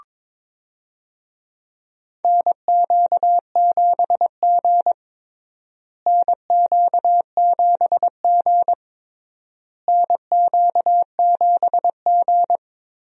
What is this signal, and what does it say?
Morse code, heard as a keyed tone, about 21 wpm, 700 Hz. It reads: NQ7G NQ7G NQ7G
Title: USA Call Signs - 4 Characters - Repeated 3x - 22wpm